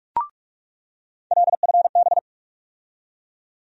Morse code, heard as a keyed tone, about 42 wpm, 700 Hz. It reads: LFB
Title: USA Call Sign Postfixes - 3 Letters 45wpm